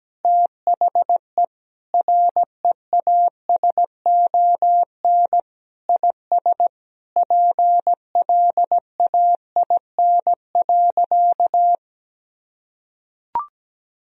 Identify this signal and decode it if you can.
Morse code, heard as a keyed tone, about 17 wpm, 700 Hz. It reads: THE REASON IS PLAIN.
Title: Sentences from Top 1000 Words - Review Part 2 - 17wpm